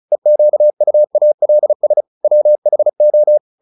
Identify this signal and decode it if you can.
Morse code, heard as a keyed tone, about 34 wpm, 600 Hz. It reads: EQUALS WHO